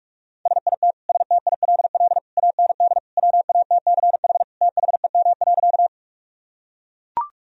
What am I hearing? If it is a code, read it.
Morse code, heard as a keyed tone, about 41 wpm, 700 Hz. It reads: SIT STILL AND WATCH THEM.
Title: Sentences from Top 1000 Words - Review Part 1 - 45wpm